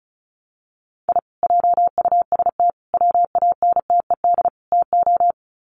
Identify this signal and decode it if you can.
Morse code, heard as a keyed tone, about 34 wpm, 700 Hz. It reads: I JUST WANTED TO
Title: Sets of 4 Words 35wpm